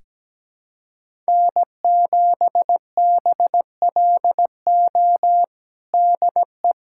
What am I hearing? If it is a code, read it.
Morse code, heard as a keyed tone, about 17 wpm, 700 Hz. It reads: N7BLO DE
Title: USA Callsign DE Callsign - 5 Characters - 17wpm